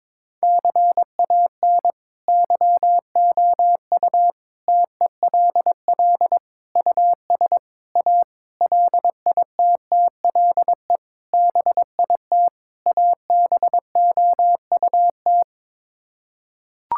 Morse code, heard as a keyed tone, about 22 wpm, 700 Hz. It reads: CAN YOU TELL US A LITTLE BIT ABOUT
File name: practice / 22wpm / Sets of 8 Words - 22wpm